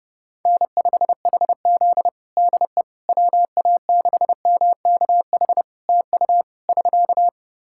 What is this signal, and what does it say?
Morse code, heard as a keyed tone, about 30 wpm, 700 Hz. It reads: N5HZ DE WA6MKH TU <SK>
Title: Practice QSO - Exchange 4 - Ending - 30wpm